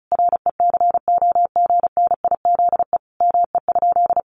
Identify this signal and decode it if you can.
Morse code, heard as a keyed tone, about 34 wpm, 700 Hz. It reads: RECOGNIZE ME?
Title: Sentences from Top 1400 Words 35wpm